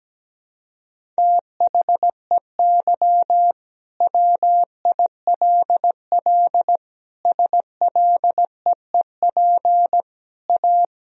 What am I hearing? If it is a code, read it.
Morse code, heard as a keyed tone, about 17 wpm, 700 Hz. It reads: THEY WILL SLEEP A
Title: Sentences from Top 1000 Words - Review Part 2 - 17wpm